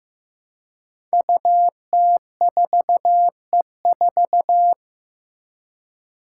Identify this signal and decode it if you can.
Morse code, heard as a keyed tone, about 15 wpm, 700 Hz. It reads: UT4E4
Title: Five Challenging Letter-Number Combinations 15wpm